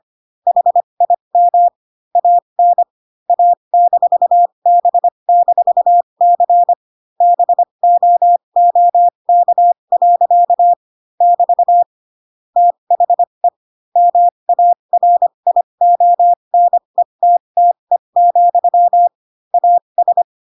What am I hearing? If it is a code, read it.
Morse code, heard as a keyed tone, about 25 wpm, spaced as for 22 wpm, 700 Hz. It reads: HIM AN A-B-C BOOK. <BT> THE MARIONETTE, AS
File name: Pinocchio / Pinocchio ch8 18 25 ik1hge